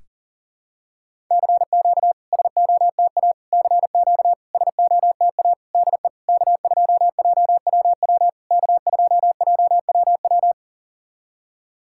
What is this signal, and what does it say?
Morse code, heard as a keyed tone, about 37 wpm, 690 Hz. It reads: CQ SOTA CQ SOTA DE K2JWW K2JWW
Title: SOTA QSO Practice 40wpm